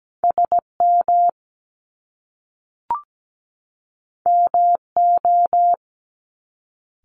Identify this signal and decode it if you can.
Morse code, heard as a keyed tone, about 17 wpm, 700 Hz. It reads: SM MO
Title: Top Two Letter Combinations in Words 17wpm